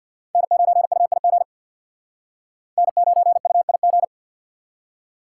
Morse code, heard as a keyed tone, about 65 wpm, 700 Hz. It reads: N9UIG N9UIG